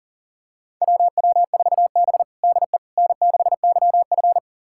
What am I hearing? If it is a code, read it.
Morse code, heard as a keyed tone, about 40 wpm, 700 Hz. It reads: WW4B DE N6YF
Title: USA Callsign DE Callsign - 4 Characters - 40wpm